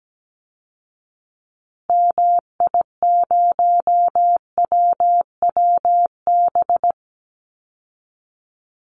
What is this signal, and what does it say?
Morse code, heard as a keyed tone, about 17 wpm, 700 Hz. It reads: MI0WWB
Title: Call Signs Repeated 3x - 17wpm